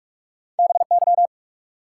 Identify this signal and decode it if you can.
Morse code, heard as a keyed tone, about 47 wpm, 700 Hz. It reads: BY
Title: Sentences from Top 100 Words - New Word Drill - 45wpm